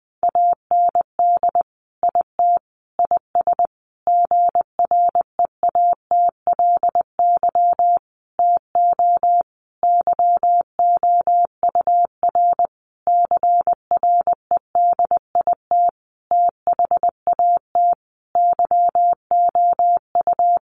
Morse code, heard as a keyed tone, about 20 wpm, 700 Hz. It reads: AND IT IS GREATLY TO YOUR CREDIT THAT YOU
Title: Sentences from Top 1400 Words 20wpm